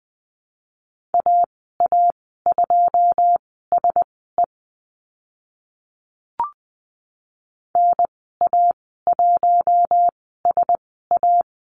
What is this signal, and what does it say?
Morse code, heard as a keyed tone, about 20 wpm, spaced as for 14 wpm, 700 Hz. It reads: AA2SE NA1SA
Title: 088 Lesson 15g - Number 5 - Call Signs 20wpm